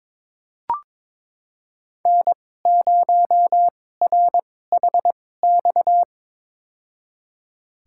Morse code, heard as a keyed tone, about 22 wpm, spaced as for 16 wpm, 700 Hz. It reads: N0RHX
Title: USA Call Signs - 5 Characters - 2x Character Spacing - 22wpm